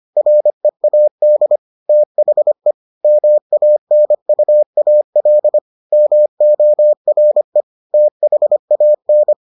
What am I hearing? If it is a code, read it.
Morse code, heard as a keyed tone, about 25 wpm, 600 Hz. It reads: READ THE MANUAL MORE THAN